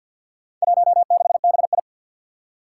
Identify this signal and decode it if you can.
Morse code, heard as a keyed tone, about 53 wpm, 700 Hz. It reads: 16BI
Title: Four Challenging Letter-Number Combinations 50wpm